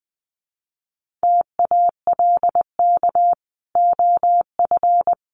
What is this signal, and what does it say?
Morse code, heard as a keyed tone, about 20 wpm, 700 Hz. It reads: TALK OF
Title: Sets of 5 Words 20wpm